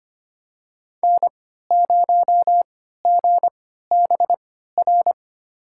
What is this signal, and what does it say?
Morse code, heard as a keyed tone, about 25 wpm, spaced as for 14 wpm, 700 Hz. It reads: N0GBR
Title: USA Call Signs - 5 Characters - 3x Character Spacing - 25wpm